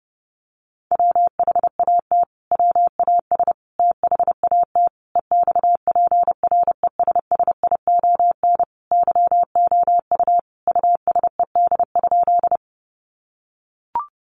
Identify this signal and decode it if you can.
Morse code, heard as a keyed tone, about 30 wpm, 700 Hz. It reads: WHAT WAS THAT EXPRESSION YOU USED?
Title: Sentences from Top 1200 Words 30wpm